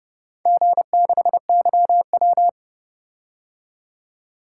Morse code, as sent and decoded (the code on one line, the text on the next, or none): --. -.... -.-- .--
G6YW